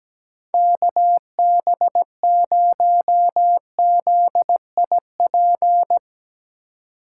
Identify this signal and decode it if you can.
Morse code, heard as a keyed tone, about 17 wpm, 700 Hz. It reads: KB0ZIP
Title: Speed Racing - USA Call Signs 17wpm